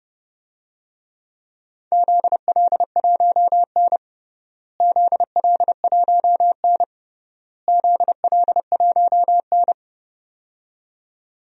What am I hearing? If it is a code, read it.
Morse code, heard as a keyed tone, about 30 wpm, 700 Hz. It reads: ZL1N ZL1N ZL1N
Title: DXCC-100 Call Signs - 4 Characters - Repeated 3x - 30wpm